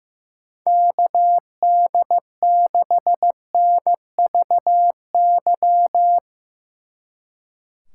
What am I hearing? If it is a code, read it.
Morse code, heard as a keyed tone, about 16 wpm, 700 Hz. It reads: KD6NVY
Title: Speed Racing - USA 2x3 Call Signs 15wpm